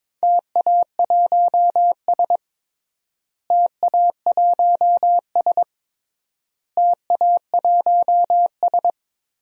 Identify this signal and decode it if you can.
Morse code, heard as a keyed tone, about 22 wpm, 700 Hz. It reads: TA1S TA1S TA1S
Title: DXCC-100 Call Signs - 4 Characters - Repeated 3x - 22wpm